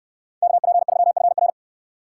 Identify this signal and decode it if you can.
Morse code, heard as a keyed tone, about 69 wpm, 700 Hz. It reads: KP3FR